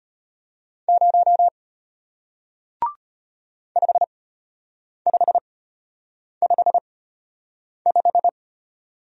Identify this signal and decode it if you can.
Morse code, heard as a keyed tone, about 33 wpm, spaced as for 8 wpm, 700 Hz. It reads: 0 5555